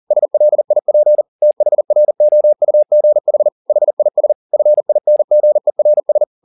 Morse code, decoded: SLIP THROUGH HIS FINGERS

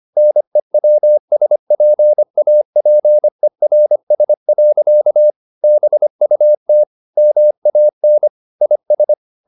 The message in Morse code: NEWSPAPERS. BUT MAN IS

